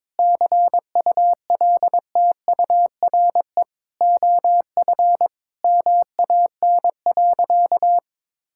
Morse code, heard as a keyed tone, about 22 wpm, 700 Hz. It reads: CULTURE OF MAN.
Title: Sentences from Top 1000 Words - Review Part 6 - 22wpm